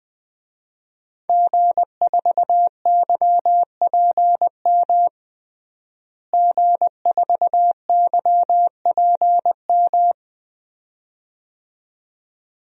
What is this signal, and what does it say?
Morse code, heard as a keyed tone, about 20 wpm, 700 Hz. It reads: G4YPM G4YPM
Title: DXCC-100 Call Signs - 5 Characters - Repeated 2x - 20wpm